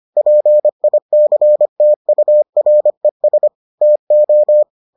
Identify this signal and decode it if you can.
Morse code, heard as a keyed tone, about 25 wpm, 600 Hz. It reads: PICTURES TO